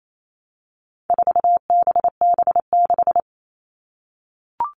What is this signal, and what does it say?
Morse code, heard as a keyed tone, about 28 wpm, 700 Hz. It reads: 4BB6
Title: All Cut Numbers - 4-Digit 28wpm